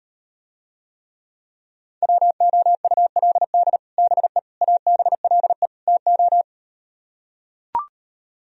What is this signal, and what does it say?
Morse code, heard as a keyed tone, about 38 wpm, 700 Hz. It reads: WOULD BE ABLE TO